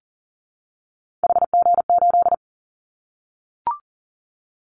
Morse code, heard as a keyed tone, about 40 wpm, 700 Hz. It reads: HG8
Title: Callsign Prefixes for Top 50 DXCC - 40wpm